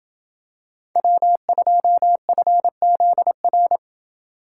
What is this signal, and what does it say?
Morse code, heard as a keyed tone, about 28 wpm, 700 Hz. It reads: W2FZR